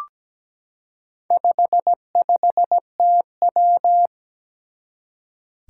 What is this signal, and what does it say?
Morse code, heard as a keyed tone, about 17 wpm, 700 Hz. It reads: 55TW
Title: Four Challenging Letter-Number Combinations 17wpm